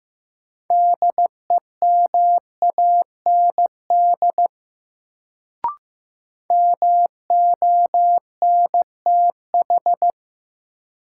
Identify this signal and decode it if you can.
Morse code, heard as a keyed tone, about 15 wpm, 700 Hz. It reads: DEMAND MONTH
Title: Sentences from Top 800 Words - New Word Drill - 15wpm